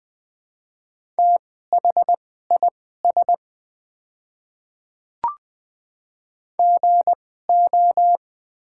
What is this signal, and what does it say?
Morse code, heard as a keyed tone, about 20 wpm, spaced as for 14 wpm, 700 Hz. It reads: THIS GO
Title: 2x Character Spacing - Top 100 Words - 20wpm